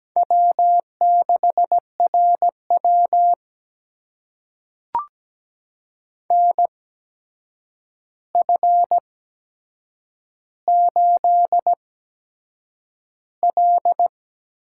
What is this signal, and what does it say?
Morse code, heard as a keyed tone, about 17 wpm, spaced as for 7 wpm, 700 Hz. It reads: W6RW NF8L
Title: USA Call Signs - 8x Character Spacing - 17wpm